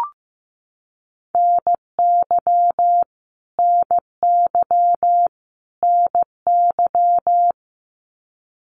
Morse code, heard as a keyed tone, about 15 wpm, 700 Hz. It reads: NY NY NY
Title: US States Repeated 3x 15wpm